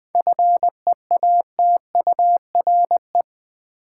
Morse code, heard as a keyed tone, about 21 wpm, 700 Hz. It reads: FEATURE
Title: Seven Letter Words - 20wpm